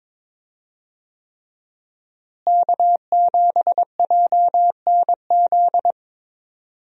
Morse code, heard as a keyed tone, about 22 wpm, 700 Hz. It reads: K7JNZ
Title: USA Call Signs - 5 Characters - Repeated 3x - 22wpm